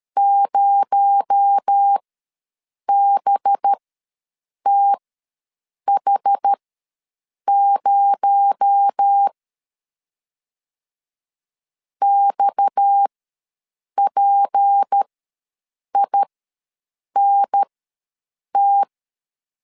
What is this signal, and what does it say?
Morse code, heard as a keyed tone, about 13 wpm, spaced as for 7 wpm, 800 Hz. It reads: 0BTH0 XPINT
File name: LEZIONI / 6 PKX0 / 6 PKX0 ws 12wpm 3448